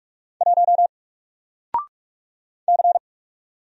Morse code, heard as a keyed tone, about 44 wpm, 700 Hz. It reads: 1 C
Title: Single Character (Letter:Number:Punctuation:Prosign) Alt 45wpm